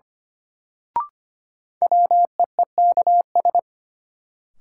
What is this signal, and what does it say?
Morse code, heard as a keyed tone, about 25 wpm, 700 Hz. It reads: WEEKS
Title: Sets of 5 Words - New Word Drill - 25wpm